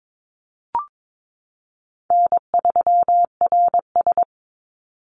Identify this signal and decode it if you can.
Morse code, heard as a keyed tone, about 22 wpm, 700 Hz. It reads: N3RS
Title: Easy Call Signs Repeated 2x - 22wpm